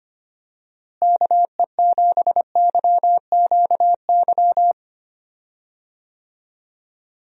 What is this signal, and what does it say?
Morse code, heard as a keyed tone, about 25 wpm, 700 Hz. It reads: KE7YQY
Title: USA Call Signs - 6 Characters - Repeated 3x - 25wpm